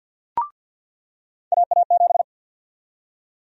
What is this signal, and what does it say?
Morse code, heard as a keyed tone, about 47 wpm, 700 Hz. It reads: AA7
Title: USA Call Sign Prefixes - 50wpm